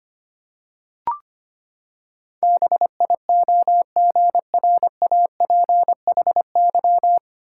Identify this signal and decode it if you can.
Morse code, heard as a keyed tone, about 25 wpm, 700 Hz. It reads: BIOGRAPHY
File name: Nine Letter Words - 25wpm